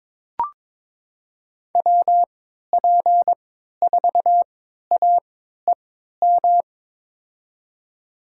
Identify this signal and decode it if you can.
Morse code, heard as a keyed tone, about 22 wpm, spaced as for 13 wpm, 700 Hz. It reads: WP4AEM
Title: USA Call Signs - 6 Characters - 3x Character Spacing - 22wpm